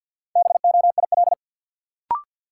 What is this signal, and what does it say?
Morse code, heard as a keyed tone, about 50 wpm, 700 Hz. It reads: BOIL
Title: Sets of 5 Words Encore - New Word Drill - 50wpm